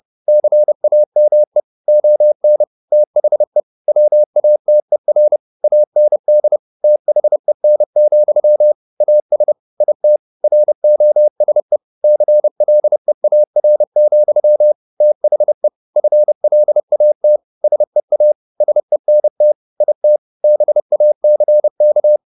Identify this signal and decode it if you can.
Morse code, heard as a keyed tone, about 30 wpm, 600 Hz. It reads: CAME ON THE WATER AND THEN, AS IT ROSE CLEAR, THE FLAT SEA SENT IT BACK
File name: the old man and the sea / the old man and the sea 1 30wpm